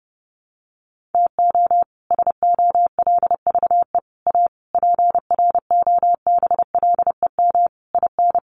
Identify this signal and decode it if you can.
Morse code, heard as a keyed tone, about 30 wpm, 700 Hz. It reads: TO SOLVE A PROBLEM IN
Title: Sentences from Top 1400 Words 30wpm